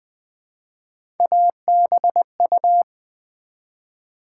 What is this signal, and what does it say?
Morse code, heard as a keyed tone, about 20 wpm, 700 Hz. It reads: ABU
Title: Build A Word - 1 Letter At A Time - 5-Letter Words - 20wpm